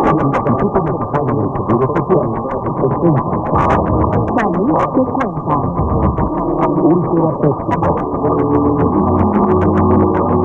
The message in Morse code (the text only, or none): none